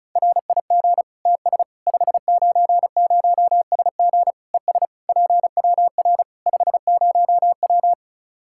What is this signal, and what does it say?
Morse code, heard as a keyed tone, about 34 wpm, 700 Hz. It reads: RIG TS 590SG ES PWR 50W
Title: K1USN SST QSO Practice 35wpm